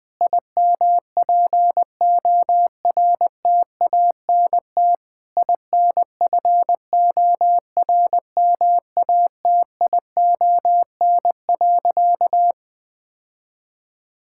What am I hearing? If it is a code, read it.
Morse code, heard as a keyed tone, about 21 wpm, 700 Hz. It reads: IMPORTANT INFORMATION.
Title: Sentences from Top 1000 Words - Review Part 6 - 20wpm